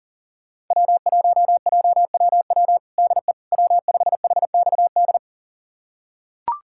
Morse code, heard as a keyed tone, about 40 wpm, 700 Hz. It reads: W1JWW DE W5HXD
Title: USA Callsign DE Callsign 40wpm